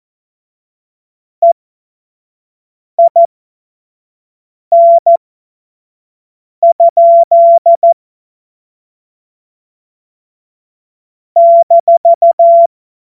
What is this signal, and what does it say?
Morse code, heard as a keyed tone, about 14 wpm, spaced as for 5 wpm, 680 Hz. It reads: EIN? -
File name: Morsekurs / Witze / Witz 18. April 2023 25 F70